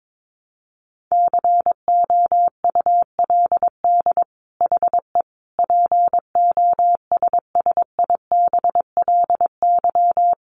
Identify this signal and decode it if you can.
Morse code, heard as a keyed tone, about 22 wpm, 700 Hz. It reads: COULD HE POSSIBLY
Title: Sentences from Top 1300 Words 22wpm